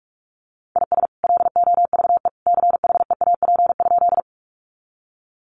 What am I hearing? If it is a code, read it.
Morse code, heard as a keyed tone, about 44 wpm, 700 Hz. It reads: IS LOVE CHEAP?
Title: Sentences from Top 1200 Words 45wpm